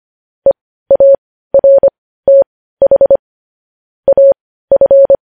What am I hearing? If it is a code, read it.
Morse code, heard as a keyed tone, about 25 wpm, spaced as for 15 wpm, 550 Hz. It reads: EARTH AF
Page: news headlines in morse code at 15 wpm